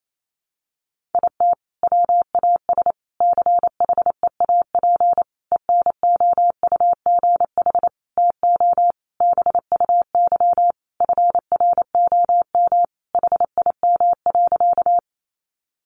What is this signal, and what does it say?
Morse code, heard as a keyed tone, about 28 wpm, 700 Hz. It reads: IT WAS CHEAP ENOUGH TO BUY FROM HIM.
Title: Sentences from Top 1200 Words 28wpm